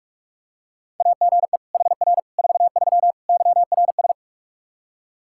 Morse code, heard as a keyed tone, about 46 wpm, 700 Hz. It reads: AGE HR 43 YRS
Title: Practice QSO - Exchange 3 - More Info - 45wpm